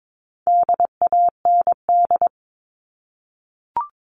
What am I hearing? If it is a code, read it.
Morse code, heard as a keyed tone, about 22 wpm, 700 Hz. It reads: DAND
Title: All Cut Numbers - 4-Digit 22wpm